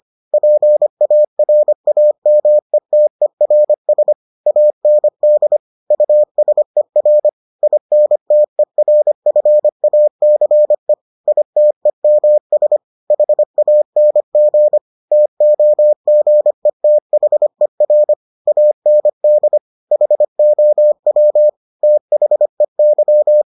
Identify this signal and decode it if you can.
Morse code, heard as a keyed tone, about 25 wpm, 600 Hz. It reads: PARAMETERS AND USER INTERFACE ITEMS HANG TOGETHER AND HOW THEY